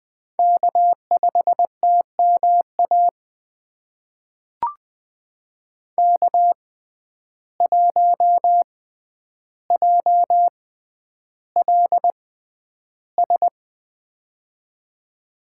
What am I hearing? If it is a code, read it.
Morse code, heard as a keyed tone, about 20 wpm, spaced as for 10 wpm, 700 Hz. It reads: K5TMA K1JLS
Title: USA Call Signs - 5 Characters - 6x Character Spacing - 20wpm